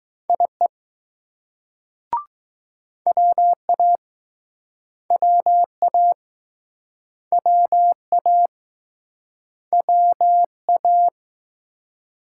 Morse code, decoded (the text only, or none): IE WA WA WA WA